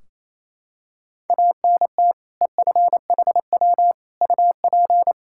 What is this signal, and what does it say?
Morse code, heard as a keyed tone, about 28 wpm, 700 Hz. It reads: ANT EFHW UP
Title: Practice QSO - Exchange 2 - Info - 28wpm